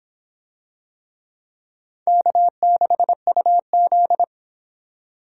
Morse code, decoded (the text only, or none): K6UZ